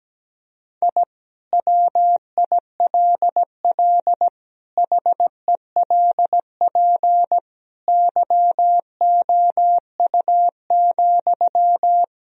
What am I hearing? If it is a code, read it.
Morse code, heard as a keyed tone, about 17 wpm, 700 Hz. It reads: I WILL HELP YOU,
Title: Sentences from Top 1000 Words - Review Part 3 - 17wpm